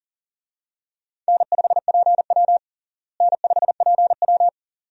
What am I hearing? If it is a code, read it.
Morse code, heard as a keyed tone, about 40 wpm, 700 Hz. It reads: N5PW N5PW
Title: USA Call Signs - 4 Characters - Repeated 2x - 40wpm